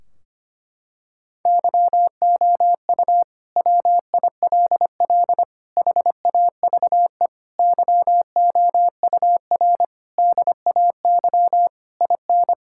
Morse code, heard as a keyed tone, about 25 wpm, 700 Hz. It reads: YOU WILL HAVE YOUR DAY IN
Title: Sentences from Top 1600 Words 25wpm